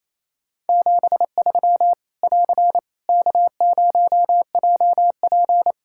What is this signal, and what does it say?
Morse code, heard as a keyed tone, about 28 wpm, 700 Hz. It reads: 73 <AR> K0JP
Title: Practice QSO - Exchange 4 - Ending - 28wpm